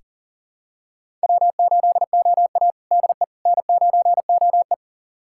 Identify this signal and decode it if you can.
Morse code, heard as a keyed tone, about 40 wpm, 700 Hz. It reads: W8OA DE N9OE
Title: USA Callsign DE Callsign - 4 Characters - 40wpm